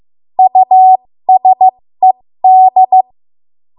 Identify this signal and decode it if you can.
Morse code, heard as a keyed tone, about 15 wpm, spaced as for 13 wpm, 750 Hz. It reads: USED
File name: Morse / 13WPM